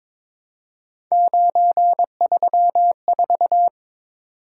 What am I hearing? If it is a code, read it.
Morse code, heard as a keyed tone, about 22 wpm, 700 Hz. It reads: 934
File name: Three Letter-Number Combinations - 22wpm